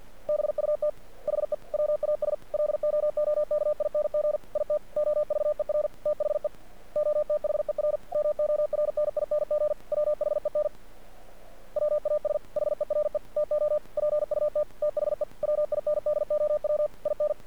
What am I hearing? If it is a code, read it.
Morse code, heard as a keyed tone, about 48 wpm, 600 Hz. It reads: BUT HE WAS LOOKING IT OVER THE OTHER MORNING WHEN WAS HERE TO PUT THE WINDOW IN
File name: archived / CRSnet